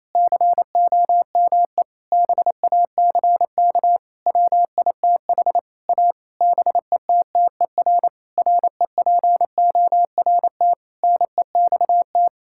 Morse code, heard as a keyed tone, about 28 wpm, 700 Hz. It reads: COME BACK WITH A BETTER REPORT NEXT
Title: Sentences from Top 1000 Words - Review Part 6 - 28wpm